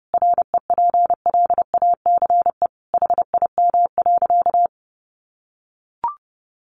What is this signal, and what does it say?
Morse code, heard as a keyed tone, about 30 wpm, 700 Hz. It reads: REPLACE HIM.
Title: Sentences from Top 1400 Words 30wpm